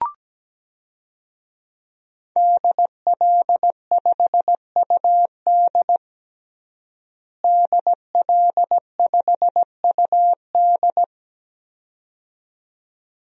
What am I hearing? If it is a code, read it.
Morse code, heard as a keyed tone, about 17 wpm, 700 Hz. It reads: DL5UD DL5UD
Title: DXCC-100 Call Signs - 5 Characters - Repeated 2x - 17wpm